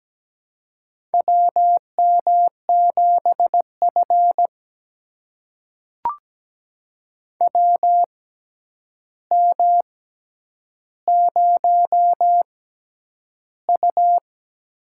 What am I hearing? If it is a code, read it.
Morse code, heard as a keyed tone, about 17 wpm, spaced as for 9 wpm, 700 Hz. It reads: WM7F WM0U
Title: USA Call Signs - 4 Characters - 6x Character Spacing - 17wpm